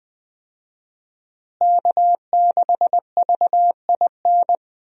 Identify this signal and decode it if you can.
Morse code, heard as a keyed tone, about 20 wpm, 700 Hz. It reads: K6VIN
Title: USA Call Signs - 5 Characters - Repeated 3x - 20wpm